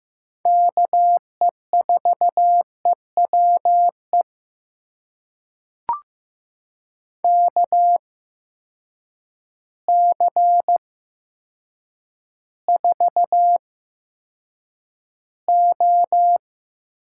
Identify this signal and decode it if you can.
Morse code, heard as a keyed tone, about 15 wpm, spaced as for 8 wpm, 700 Hz. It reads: KE4EWE KC4O
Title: USA Call Signs - 6 Characters - 8x Character Spacing - 15wpm